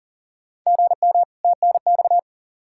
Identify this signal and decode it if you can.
Morse code, heard as a keyed tone, about 37 wpm, 700 Hz. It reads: GM TNX
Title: Practice QSO - Exchange 1 - Beginning - 40wpm